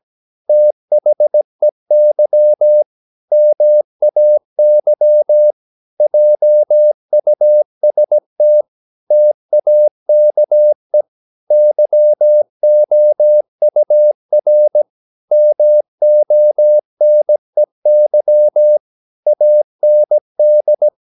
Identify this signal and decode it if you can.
Morse code, heard as a keyed tone, about 17 wpm, 600 Hz. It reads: THEY MAY JUST TAKE YOUR MONEY AND
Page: Romanen Never af Ken Follet på engelsk.